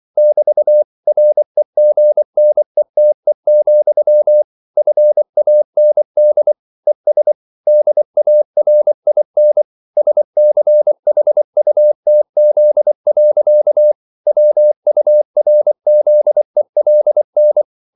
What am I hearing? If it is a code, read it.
Morse code, heard as a keyed tone, about 24 wpm, 600 Hz. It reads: <BT> REGNETE, FAND ES DARIN SCHUTZ. WURZELN